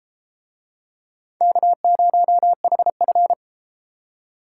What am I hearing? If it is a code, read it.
Morse code, heard as a keyed tone, about 31 wpm, 700 Hz. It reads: K0HF